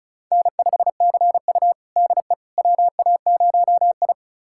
Morse code, heard as a keyed tone, about 36 wpm, 700 Hz. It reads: N5CU DE WA0I
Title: USA Callsign DE Callsign - 4 Characters - 35wpm